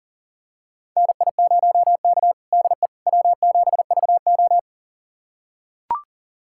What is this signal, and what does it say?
Morse code, heard as a keyed tone, about 40 wpm, 690 Hz. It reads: NI0K DE W7VO
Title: USA Callsign DE Callsign - 4 Characters - 40wpm